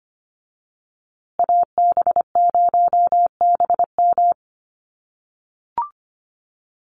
Morse code, heard as a keyed tone, about 25 wpm, 700 Hz. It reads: AB0BM
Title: USA Call Signs Repeated 2x 25wpm